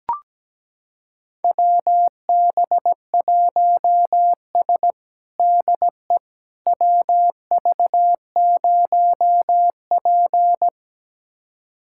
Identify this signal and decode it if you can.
Morse code, heard as a keyed tone, about 17 wpm, 700 Hz. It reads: WB1S DE WV0P
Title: USA Callsign DE Callsign - 4 Characters - 17wpm